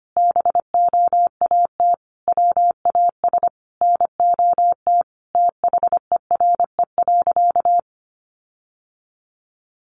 Morse code, heard as a keyed tone, about 25 wpm, 700 Hz. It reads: BOAT WAS NOT THERE.
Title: Sentences from Top 1400 Words 25wpm